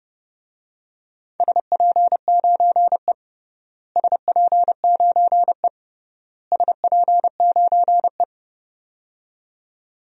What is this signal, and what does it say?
Morse code, heard as a keyed tone, about 30 wpm, 700 Hz. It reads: SP9E SP9E SP9E
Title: DXCC-100 Call Signs - 4 Characters - Repeated 3x - 30wpm